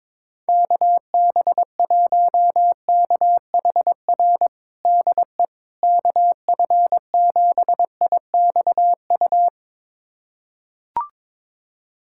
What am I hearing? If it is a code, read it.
Morse code, heard as a keyed tone, about 22 wpm, 700 Hz. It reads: KB1KHR DE KF7IXU
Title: USA Callsign DE Callsign - 6 Characters - 22wpm